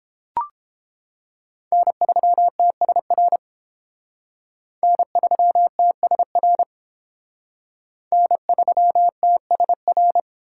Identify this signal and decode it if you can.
Morse code, heard as a keyed tone, about 30 wpm, 700 Hz. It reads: N3TSR N3TSR N3TSR